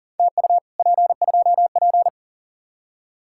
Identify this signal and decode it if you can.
Morse code, heard as a keyed tone, about 52 wpm, 700 Hz. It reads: TU P2P